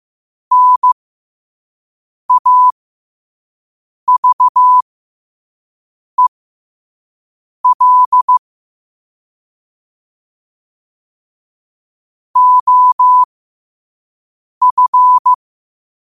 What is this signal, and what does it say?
Morse code, heard as a keyed tone, about 15 wpm, spaced as for 5 wpm, 1000 Hz. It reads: NAVEL OF
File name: qotdfiles / 5wpm